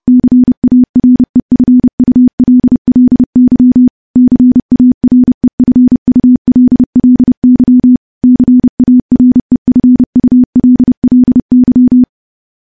Morse code, heard as a keyed tone, about 31 wpm, 250 Hz. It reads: CAREFULLY CAREFULLY CAREFULLY
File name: morse / morse-training-30WPM